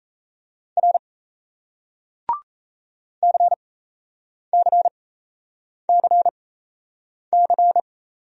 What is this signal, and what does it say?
Morse code, heard as a keyed tone, about 33 wpm, spaced as for 8 wpm, 700 Hz. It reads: R CCCC
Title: Speed-Racing Single Letters 28wpm